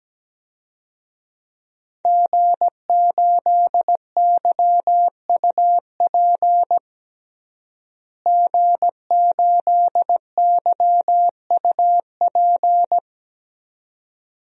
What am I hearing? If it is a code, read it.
Morse code, heard as a keyed tone, about 17 wpm, 700 Hz. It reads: G8YUP G8YUP
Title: DXCC-100 Call Signs - 5 Characters - Repeated 2x - 17wpm